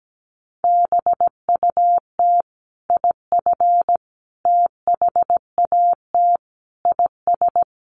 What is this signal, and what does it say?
Morse code, heard as a keyed tone, about 17 wpm, 700 Hz. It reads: BUT IF THAT IS
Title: Sentences from Top 200 Words 17wpm